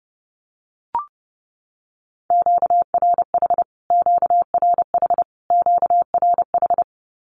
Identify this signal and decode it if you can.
Morse code, heard as a keyed tone, about 28 wpm, 700 Hz. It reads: QRH QRH QRH
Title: Q-Codes Repeated 3x 30wpm